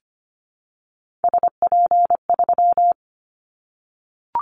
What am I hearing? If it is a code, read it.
Morse code, heard as a keyed tone, about 25 wpm, 700 Hz. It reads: SP3
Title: Callsign Prefixes for Top 25 DXCC - 25wpm